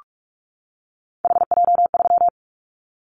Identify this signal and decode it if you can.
Morse code, heard as a keyed tone, about 44 wpm, 700 Hz. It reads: HJ3
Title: Callsign Prefixes for Top 100 DXCC - 45wpm